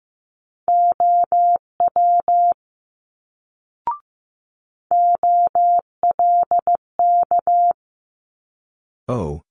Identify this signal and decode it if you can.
Morse code, heard as a keyed tone, about 15 wpm, 700 Hz. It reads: OW OLK
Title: Multi-letter Phonemes Quick Fire 15wpm